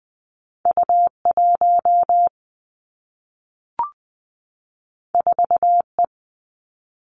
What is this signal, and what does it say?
Morse code, heard as a keyed tone, about 20 wpm, 700 Hz. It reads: U 1 4 E
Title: Two Challenging Letter-Number Combinations 20wpm